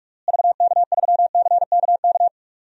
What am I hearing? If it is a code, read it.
Morse code, heard as a keyed tone, about 46 wpm, 700 Hz. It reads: VK3CKK
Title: DXCC-100 Call Signs - 6 Characters - Repeated 3x - 45wpm